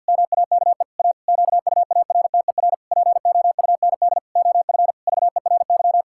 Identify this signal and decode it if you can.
Morse code, heard as a keyed tone, about 53 wpm, 700 Hz. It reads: MAKE A QUARTER POUND OF VERY
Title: Pangram Challenge v2 - 50wpm